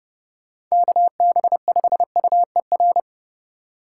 Morse code, heard as a keyed tone, about 37 wpm, 700 Hz. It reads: KB5UER
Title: Speed Racing - USA 2x3 Call Signs 30wpm